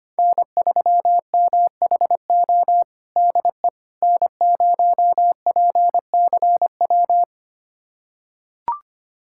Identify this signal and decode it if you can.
Morse code, heard as a keyed tone, about 25 wpm, 700 Hz. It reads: N3MHO DE N0PCW
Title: USA Callsign DE Callsign - 5 Characters - 25wpm